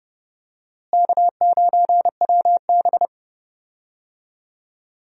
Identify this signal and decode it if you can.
Morse code, heard as a keyed tone, about 30 wpm, 700 Hz. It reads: K9WB
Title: USA Call Signs - 4 Characters - Repeated 2x - 30wpm